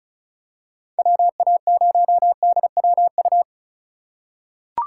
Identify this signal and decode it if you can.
Morse code, heard as a keyed tone, about 34 wpm, 700 Hz. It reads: WA0DWU
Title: USA Call Signs - 6 Characters - 2x Character Spacing - 35wpm